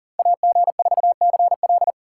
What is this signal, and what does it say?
Morse code, heard as a keyed tone, about 49 wpm, 700 Hz. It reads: AG4CL
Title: Speed Racing - USA 2x2 Call Signs 40wpm